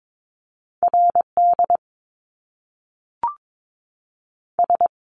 Morse code, heard as a keyed tone, about 22 wpm, 700 Hz. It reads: RD S
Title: Top Two Letter Combinations in Words 22wpm